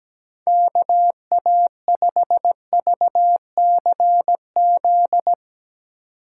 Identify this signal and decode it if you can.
Morse code, heard as a keyed tone, about 17 wpm, 700 Hz. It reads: KA5VCZ
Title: USA Call Signs - 4x Character Spacing - 17wpm